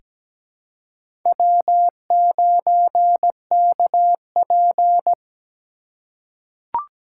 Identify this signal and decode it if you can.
Morse code, heard as a keyed tone, about 17 wpm, 700 Hz. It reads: W9KP
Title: USA Call Signs - 3x Character Spacing - 17wpm